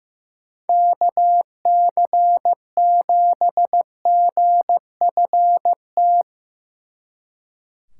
Morse code, heard as a keyed tone, about 17 wpm, 700 Hz. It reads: KC7GFT
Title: Speed Racing - USA 2x3 Call Signs 15wpm